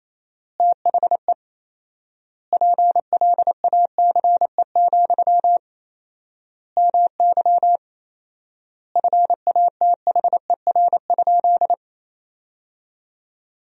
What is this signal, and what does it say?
Morse code, heard as a keyed tone, about 28 wpm, 700 Hz. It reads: THE PLACE, MY FATHER?
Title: 4x Word-Spacing - Sentences from Top 400 Words - 28wpm